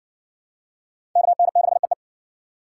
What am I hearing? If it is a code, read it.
Morse code, heard as a keyed tone, about 60 wpm, 700 Hz. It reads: KN6EE